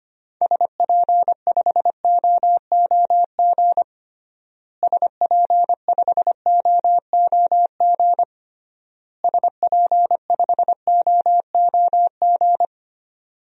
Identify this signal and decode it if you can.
Morse code, heard as a keyed tone, about 25 wpm, 700 Hz. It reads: SP5OOG SP5OOG SP5OOG
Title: DXCC-100 Call Signs - 6 Characters - Repeated 3x - 25wpm